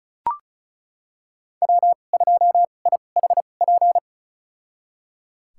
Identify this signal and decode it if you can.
Morse code, heard as a keyed tone, about 36 wpm, spaced as for 25 wpm, 700 Hz. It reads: W2IHP
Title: USA Call Signs - 2x Character Spacing - 35wpm